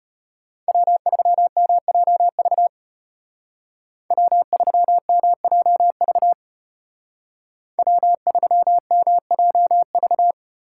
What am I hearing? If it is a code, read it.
Morse code, heard as a keyed tone, about 34 wpm, 700 Hz. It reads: W3MJV W3MJV W3MJV